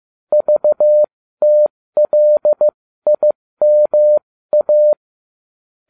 Morse code, heard as a keyed tone, about 15 wpm, spaced as for 13 wpm, 600 Hz. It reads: VTLIMA
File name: cw / ubacw20 2